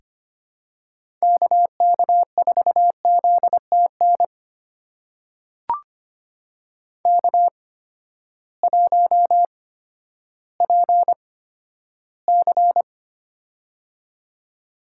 Morse code, heard as a keyed tone, about 25 wpm, spaced as for 12 wpm, 700 Hz. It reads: KK4ZTN K1PC
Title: USA Call Signs - 8x Character Spacing - 25wpm